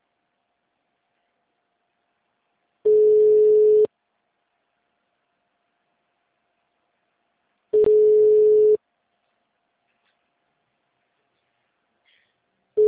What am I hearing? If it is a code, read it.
Ringback tone (Europe, 425 Hz).